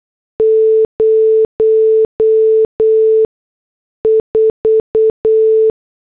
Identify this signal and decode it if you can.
Morse code, heard as a keyed tone, about 8 wpm, spaced as for 6 wpm, 440 Hz. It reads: 04